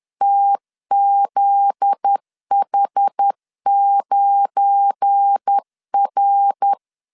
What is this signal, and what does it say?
Morse code, heard as a keyed tone, about 10 wpm, 800 Hz. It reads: TZH9R
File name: CW / DET / 1 9 1500C S B 10wpm 4439